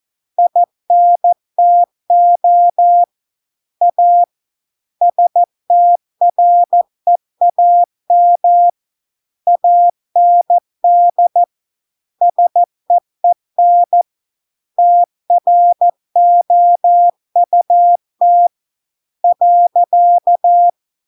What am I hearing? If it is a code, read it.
Morse code, heard as a keyed tone, about 14 wpm, 700 Hz. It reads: INTO A STREAM AND SEEN TROUT .